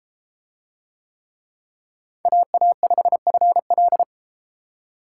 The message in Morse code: AA5FL

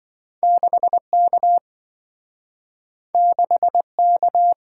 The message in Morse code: 6K 6K

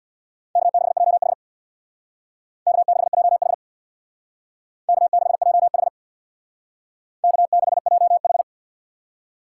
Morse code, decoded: K6JH K6JH K6JH K6JH

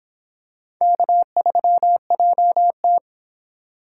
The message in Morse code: K3JT

